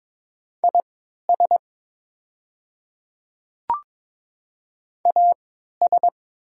Morse code, heard as a keyed tone, about 22 wpm, spaced as for 12 wpm, 700 Hz. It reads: IS AS
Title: 3x Character Spacing - Top 100 Words - 22wpm